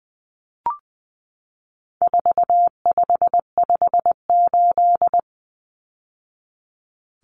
Morse code, heard as a keyed tone, about 20 wpm, 700 Hz. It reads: 4558
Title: Numbers - 4-Digit 20wpm